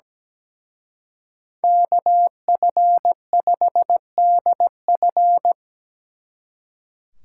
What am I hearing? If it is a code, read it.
Morse code, heard as a keyed tone, about 20 wpm, 700 Hz. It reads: KF5DF
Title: Speed Racing - USA 2x2 Call Signs 17wpm